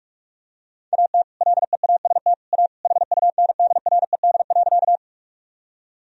Morse code, heard as a keyed tone, about 42 wpm, 700 Hz. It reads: AT LEAST A HUNDRED.
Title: Sentences from Top 1000 Words - Review Part 2 - 45wpm